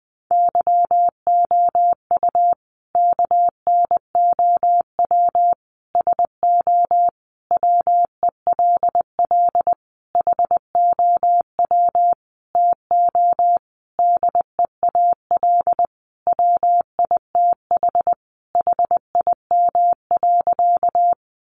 Morse code, heard as a keyed tone, about 20 wpm, 700 Hz. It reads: YOU KNOW SO WELL HOW TO DEAL WITH HIM.
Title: Sentences from Top 800 Words 20wpm